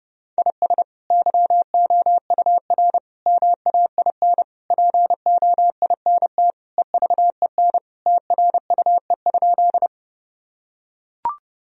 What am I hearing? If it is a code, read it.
Morse code, heard as a keyed tone, about 28 wpm, 700 Hz. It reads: IS YOUR MAIN POINT EVEN TRUE?
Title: Sentences from Top 1000 Words - Review Part 2 - 30wpm